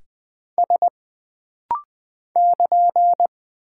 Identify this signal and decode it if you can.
Morse code, heard as a keyed tone, about 20 wpm, 700 Hz. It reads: S <KN>
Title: Single Character (Letter:Number:Punctuation:Prosign) - Rapid-Fire - 20wpm